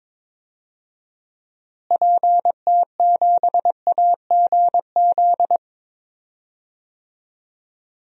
Morse code, heard as a keyed tone, about 22 wpm, 700 Hz. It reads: PT7AGZ
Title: DXCC-100 Call Signs - 6 Characters - Repeated 3x - 22wpm